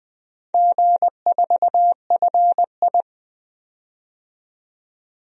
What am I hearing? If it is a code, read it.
Morse code, heard as a keyed tone, about 20 wpm, 700 Hz. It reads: G4FI
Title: DXCC-100 Call Signs - 4 Characters - 20wpm